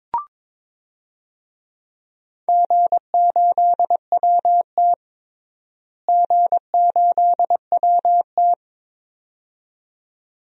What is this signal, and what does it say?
Morse code, heard as a keyed tone, about 22 wpm, 700 Hz. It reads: G8WT G8WT
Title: DXCC-100 Call Signs - 4 Characters - Repeated 2x - 22wpm